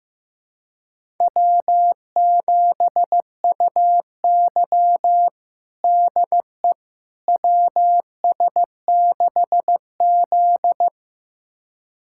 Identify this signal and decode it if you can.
Morse code, heard as a keyed tone, about 15 wpm, 700 Hz. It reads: W7UY DE WS6Z
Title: USA Callsign DE Callsign - 4 Characters - 15wpm